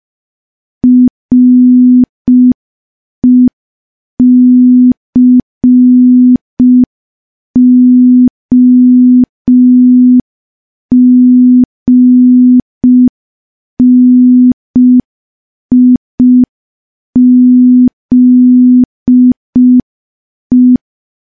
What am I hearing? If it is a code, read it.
Morse code, heard as a keyed tone, about 5 wpm, 250 Hz. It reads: RECOGNIZE
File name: morse / morse-training-05WPM